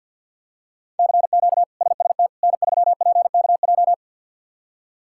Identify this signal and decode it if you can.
Morse code, heard as a keyed tone, about 52 wpm, 700 Hz. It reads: CQ SST N3PKJ
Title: K1USN SST QSO Practice - 50wpm